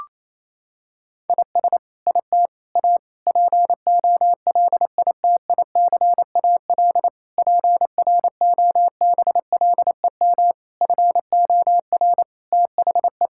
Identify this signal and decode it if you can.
Morse code, heard as a keyed tone, about 28 wpm, 700 Hz. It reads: IS IT A POLITICAL PROBLEM FOR THE
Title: Sentences from Top 1000 Words - Review Part 5 - 28wpm